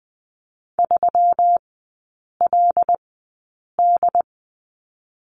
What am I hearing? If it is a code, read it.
Morse code, heard as a keyed tone, about 20 wpm, spaced as for 8 wpm, 690 Hz. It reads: 3LD
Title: 172 Lesson 28e - Number 9 - Three Characters 20wpm